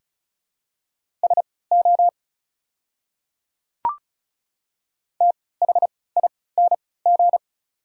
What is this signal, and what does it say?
Morse code, heard as a keyed tone, about 36 wpm, spaced as for 20 wpm, 700 Hz. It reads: SO THING
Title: 3x Character Spacing - Top 100 Words - 35wpm